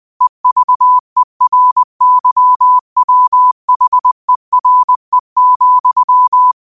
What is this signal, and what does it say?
Morse code, heard as a keyed tone, about 20 wpm, 1000 Hz. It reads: EVERYWHERE,
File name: qotdfiles / 20wpm